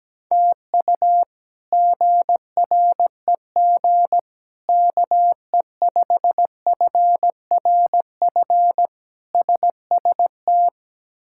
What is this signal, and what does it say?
Morse code, heard as a keyed tone, about 17 wpm, 700 Hz. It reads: TU GREG KE5FRF SST
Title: K1USN SST QSO Practice - 17wpm